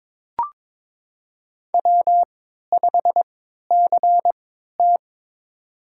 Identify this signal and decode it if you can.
Morse code, heard as a keyed tone, about 22 wpm, spaced as for 13 wpm, 700 Hz. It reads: W5CT
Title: USA Call Signs - 4 Characters - 3x Character Spacing - 22wpm